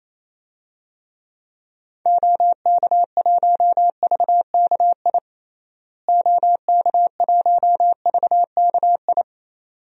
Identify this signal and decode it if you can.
Morse code, heard as a keyed tone, about 28 wpm, 700 Hz. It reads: OK1VKI OK1VKI
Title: DXCC-100 Call Signs - 6 Characters - Repeated 2x - 28wpm